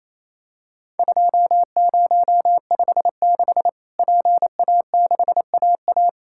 Morse code, heard as a keyed tone, about 28 wpm, 700 Hz. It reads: 2056 PA6AA
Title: Worked All Europe DX Contest - QTC Exchange - 28wpm